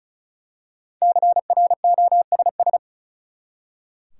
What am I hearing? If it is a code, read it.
Morse code, heard as a keyed tone, about 36 wpm, 700 Hz. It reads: CROSS
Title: Five Letter Words - 35wpm